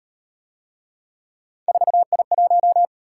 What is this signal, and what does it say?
Morse code, heard as a keyed tone, about 35 wpm, 700 Hz. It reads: 4I1